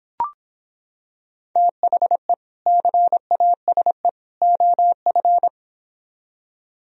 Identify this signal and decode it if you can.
Morse code, heard as a keyed tone, about 26 wpm, 700 Hz. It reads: THE CASE OF